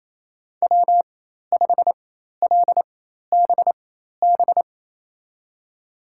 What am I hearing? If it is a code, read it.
Morse code, heard as a keyed tone, about 28 wpm, spaced as for 13 wpm, 700 Hz. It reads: W5LBB
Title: USA Call Signs - 5 Characters - 4x Character Spacing - 28wpm